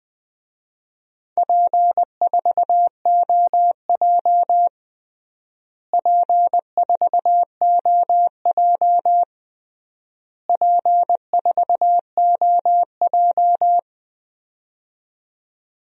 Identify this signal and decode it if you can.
Morse code, heard as a keyed tone, about 21 wpm, 700 Hz. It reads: P4OJ P4OJ P4OJ
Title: DXCC-100 Call Signs - 4 Characters - Repeated 3x - 20wpm